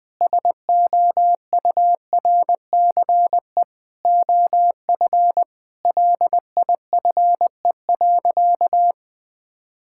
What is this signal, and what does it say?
Morse code, heard as a keyed tone, about 20 wpm, 700 Hz. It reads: SOURCE OF LIFE.
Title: Sentences from Top 1000 Words - Review Part 5 - 20wpm